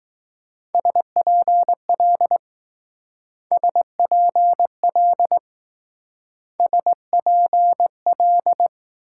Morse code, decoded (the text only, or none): SPL SPL SPL